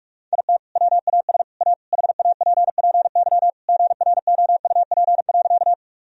Morse code, heard as a keyed tone, about 46 wpm, 700 Hz. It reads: IT WAS A HAPPY GROUP.
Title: Sentences from Top 1000 Words - Review Part 1 - 45wpm